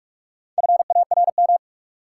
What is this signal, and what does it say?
Morse code, heard as a keyed tone, about 46 wpm, 700 Hz. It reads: FARM
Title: Sentences from Top 1200 Words - New Word Drill Alt - 45wpm